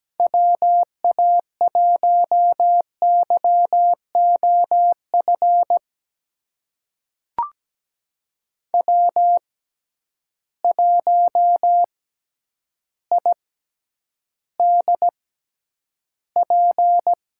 Morse code, heard as a keyed tone, about 17 wpm, spaced as for 9 wpm, 700 Hz. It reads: WA1YOF W1IDP
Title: USA Call Signs - 6x Character Spacing - 17wpm